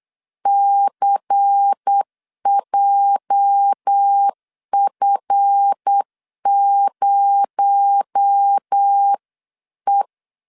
Morse code, heard as a keyed tone, about 8 wpm, 800 Hz. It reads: CJF0E
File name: CW / DET / 1 9 1500C S C 8wpm 5653